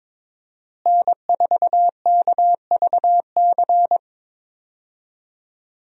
Morse code, heard as a keyed tone, about 22 wpm, 700 Hz. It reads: N4KVC
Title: USA Call Signs - 5 Characters - Repeated 3x - 22wpm